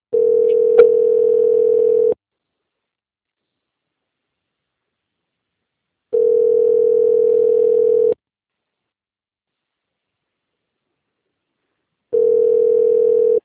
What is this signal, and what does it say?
Ringback tone (North America), 440 + 480 Hz.
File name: out- - - - - .3389